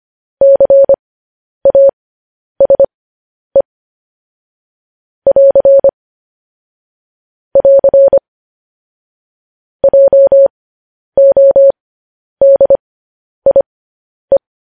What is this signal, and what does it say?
Morse code, heard as a keyed tone, about 25 wpm, spaced as for 10 wpm, 550 Hz. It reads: CASE <AR> <AR> JODIE